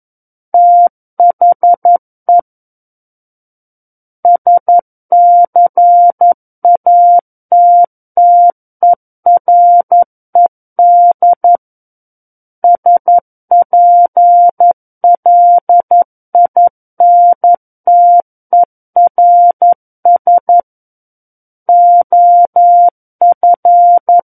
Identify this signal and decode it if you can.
Morse code, heard as a keyed tone, about 11 wpm, 700 Hz. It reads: THE SCATTERED SPLINTERS OF
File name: 11 WPM